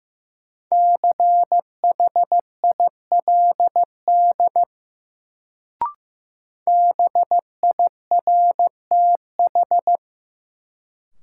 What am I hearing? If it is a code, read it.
Morse code, heard as a keyed tone, about 15 wpm, 700 Hz. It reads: CHILD BIRTH
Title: Sentences from Top 800 Words - New Word Drill - 15wpm